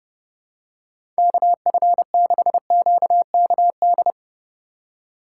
Morse code, heard as a keyed tone, about 30 wpm, 700 Hz. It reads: KF6QKD
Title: USA Call Signs - 6 Characters - 2x Character Spacing - 30wpm